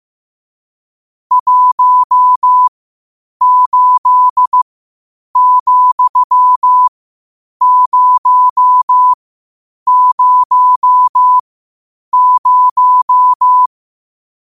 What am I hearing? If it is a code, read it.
Morse code, heard as a keyed tone, about 15 wpm, spaced as for 8 wpm, 1000 Hz. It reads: 18,000
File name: qotdfiles / 8wpm